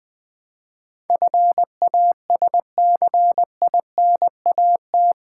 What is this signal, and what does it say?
Morse code, heard as a keyed tone, about 20 wpm, 700 Hz. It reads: FASCINAT
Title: Build A Word - 4 Letters At A Time - 11-Letter Words - 20wpm